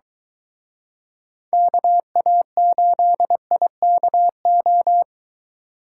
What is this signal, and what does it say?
Morse code, heard as a keyed tone, about 22 wpm, 700 Hz. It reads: KA8IKO